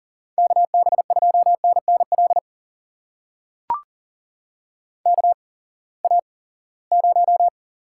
Morse code, heard as a keyed tone, about 42 wpm, spaced as for 23 wpm, 700 Hz. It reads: KB2NNL KA0
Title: USA Call Signs - 6 Characters - 8x Character Spacing - 40wpm